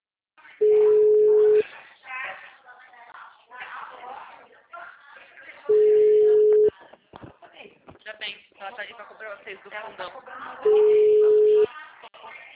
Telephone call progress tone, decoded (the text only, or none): ringback tone